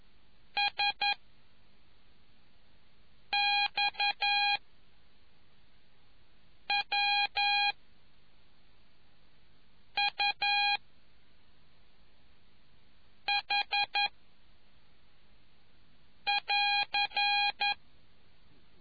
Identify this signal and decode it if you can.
Morse code, heard as a keyed tone, about 11 wpm, 2450 Hz. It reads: S X W U H <AR>